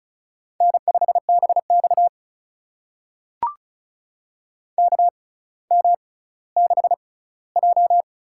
Morse code, heard as a keyed tone, about 36 wpm, spaced as for 18 wpm, 700 Hz. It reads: N5BX KM6J
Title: USA Call Signs - 4 Characters - 6x Character Spacing - 35wpm